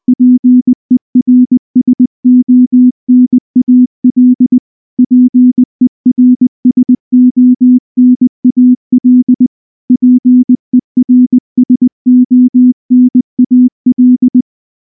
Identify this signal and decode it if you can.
Morse code, heard as a keyed tone, about 20 wpm, 250 Hz. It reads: PERSONAL PERSONAL PERSONAL
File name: morse-training-20WPM